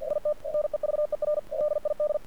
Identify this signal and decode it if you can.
Morse code, heard as a keyed tone, about 53 wpm, 600 Hz. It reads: IT NEVER DID
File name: archived / CRSnet